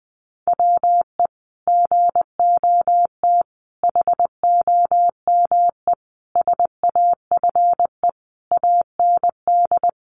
Morse code, heard as a keyed tone, about 20 wpm, 700 Hz. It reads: WE GOT HOME SAFE AND
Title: Sentences from Top 900 Words 20wpm